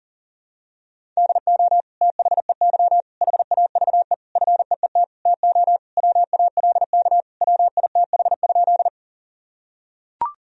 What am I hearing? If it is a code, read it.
Morse code, heard as a keyed tone, about 40 wpm, 700 Hz. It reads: DO THEY HAVE FEET TO WALK WITH?
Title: Sentences from Top 1000 Words - Review Part 1 - 40wpm